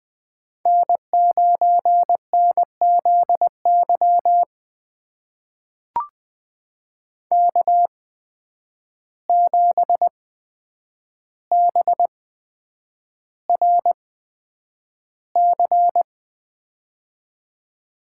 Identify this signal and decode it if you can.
Morse code, heard as a keyed tone, about 20 wpm, spaced as for 9 wpm, 700 Hz. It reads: N9NZY K7BRC
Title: USA Call Signs - 8x Character Spacing - 20wpm